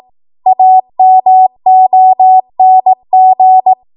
Morse code, heard as a keyed tone, about 18 wpm, 750 Hz. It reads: AMONG
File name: Morse / 18WPM